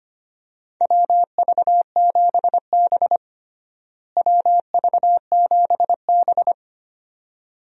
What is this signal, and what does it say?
Morse code, heard as a keyed tone, about 25 wpm, 700 Hz. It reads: WV7B WV7B
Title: USA Call Signs - 4 Characters - Repeated 2x - 25wpm